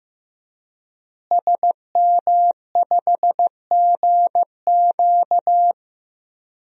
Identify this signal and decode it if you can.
Morse code, heard as a keyed tone, about 15 wpm, 700 Hz. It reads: SM5GQ
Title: DXCC-100 Call Signs - 5 Characters - Repeated 3x - 15wpm